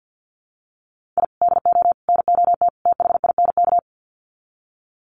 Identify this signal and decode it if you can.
Morse code, heard as a keyed tone, about 50 wpm, 700 Hz. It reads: I DO NOT THINK
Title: Sets of 4 Words 50wpm